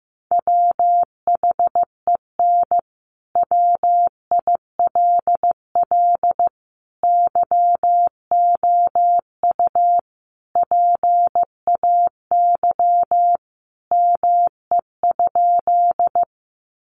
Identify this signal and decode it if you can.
Morse code, heard as a keyed tone, about 15 wpm, 700 Hz. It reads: WHEN WILL YOU PAY ME?
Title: Sentences from Top 700 Words 15wpm